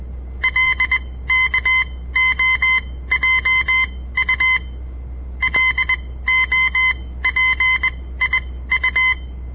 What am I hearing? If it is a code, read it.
Morse code, heard as a keyed tone, about 20 wpm, spaced as for 15 wpm, 2050 Hz. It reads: LKOJU LOPIU
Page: On the air Morse code practice from the Southern Montana Amatuer Radio Association; all code, no read back